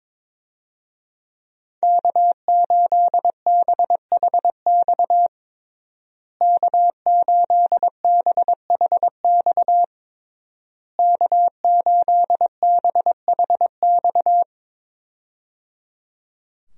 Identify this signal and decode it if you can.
Morse code, heard as a keyed tone, about 22 wpm, 700 Hz. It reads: K8BHX K8BHX K8BHX
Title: USA Call Signs - 5 Characters - Repeated 3x - 22wpm